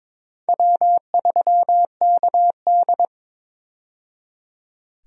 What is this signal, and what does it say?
Morse code, heard as a keyed tone, about 21 wpm, 700 Hz. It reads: W3KD
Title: USA Call Signs - 4 Characters - 22wpm